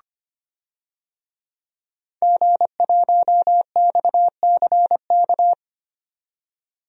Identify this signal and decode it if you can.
Morse code, heard as a keyed tone, about 25 wpm, 700 Hz. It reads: G1XCK
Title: DXCC-100 Call Signs - 5 Characters - 25wpm